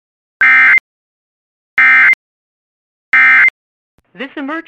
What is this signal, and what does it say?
Emergency Alert System (SAME) tones, heard as harsh buzzing bursts, with speech. It is heard from 0.4 to 3.5 s.